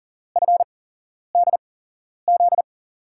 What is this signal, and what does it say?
Morse code, heard as a keyed tone, about 39 wpm, spaced as for 11 wpm, 700 Hz. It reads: FDZ
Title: USA Call Signs - 6 Characters - 8x Character Spacing - 40wpm